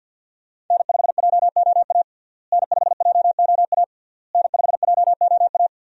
Morse code, heard as a keyed tone, about 51 wpm, 700 Hz. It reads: N5JOA N5JOA N5JOA
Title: USA Call Signs - 5 Characters - Repeated 3x - 50wpm